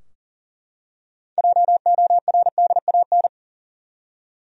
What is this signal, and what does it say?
Morse code, heard as a keyed tone, about 40 wpm, 700 Hz. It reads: JORDAN
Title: USA Common Names - 40wpm